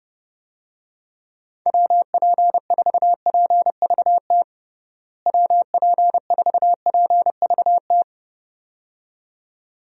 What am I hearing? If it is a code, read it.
Morse code, heard as a keyed tone, about 32 wpm, 700 Hz. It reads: WP4PVT WP4PVT
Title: USA Call Signs - 6 Characters - Repeated 2x - 30wpm